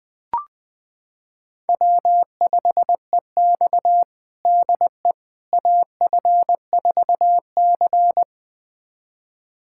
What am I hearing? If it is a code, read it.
Morse code, heard as a keyed tone, about 20 wpm, 700 Hz. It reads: W5EX DE AF4C
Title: USA Callsign DE Callsign - 4 Characters - 20wpm